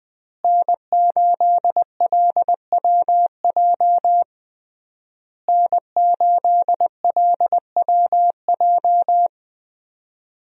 Morse code, heard as a keyed tone, about 20 wpm, 700 Hz. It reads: N8LWJ N8LWJ
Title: USA Call Signs - 5 Characters - Repeated 3x - 20wpm